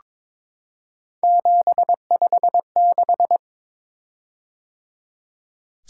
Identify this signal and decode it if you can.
Morse code, heard as a keyed tone, about 22 wpm, 700 Hz. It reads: 756
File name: practice / 22wpm / Three Letter-Number Combinations - 22wpm